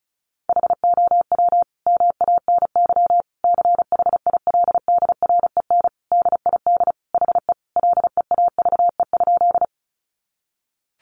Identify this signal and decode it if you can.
Morse code, heard as a keyed tone, about 34 wpm, 700 Hz. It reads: HOW MANY CHILDREN DID HE LEAVE?
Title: Sentences from Top 400 Words 35wpm